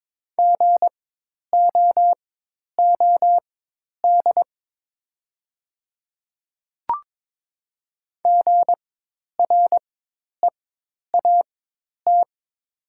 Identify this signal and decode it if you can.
Morse code, heard as a keyed tone, about 22 wpm, spaced as for 10 wpm, 690 Hz. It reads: GOOD GREAT
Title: 4x Character Spacing - Top 300 Words - 22wpm